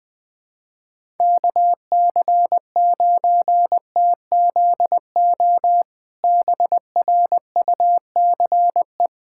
Morse code, heard as a keyed tone, about 20 wpm, 700 Hz. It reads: KC9TZO BRUCE
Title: K1USN SST QSO Practice - 20wpm